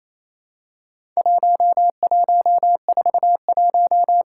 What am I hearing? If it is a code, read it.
Morse code, heard as a keyed tone, about 27 wpm, 700 Hz. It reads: 1141
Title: Worked All Europe DX Contest - QTC Exchange - 28wpm